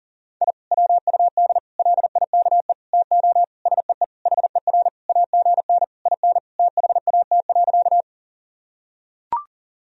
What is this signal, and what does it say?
Morse code, heard as a keyed tone, about 40 wpm, 700 Hz. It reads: I WUD LIKE TO SEE HER AGN IN THAT.
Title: Sentences from Top 200 Words and Abbreviations 40wpm